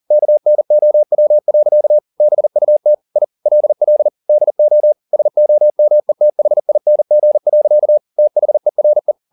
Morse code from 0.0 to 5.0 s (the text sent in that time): KNOW. BUT I LL DO